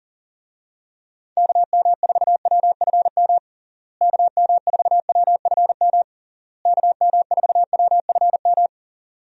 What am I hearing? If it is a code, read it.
Morse code, heard as a keyed tone, about 40 wpm, 700 Hz. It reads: KM4WFM KM4WFM KM4WFM
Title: USA Call Signs - 6 Characters - Repeated 3x - 40wpm